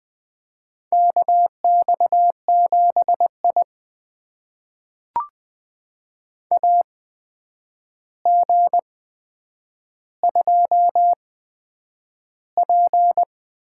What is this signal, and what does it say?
Morse code, heard as a keyed tone, about 20 wpm, spaced as for 9 wpm, 700 Hz. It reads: KX7I AG2P
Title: USA Call Signs - 4 Characters - 8x Character Spacing - 20wpm